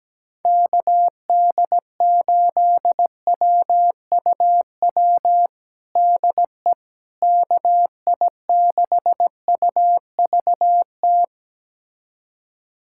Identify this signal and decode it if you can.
Morse code, heard as a keyed tone, about 17 wpm, 700 Hz. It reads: KD8WUW DE KI6UVT
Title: USA Callsign DE Callsign - 6 Characters - 17wpm